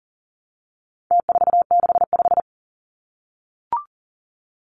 Morse code, heard as a keyed tone, about 40 wpm, 700 Hz. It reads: T465
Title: Numbers - Contest Style - 4-Digit 40wpm